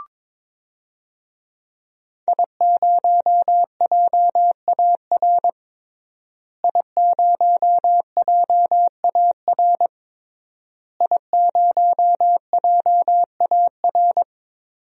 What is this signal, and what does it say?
Morse code, heard as a keyed tone, about 22 wpm, 700 Hz. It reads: I0JAR I0JAR I0JAR
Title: DXCC-100 Call Signs - 5 Characters - Repeated 3x - 22wpm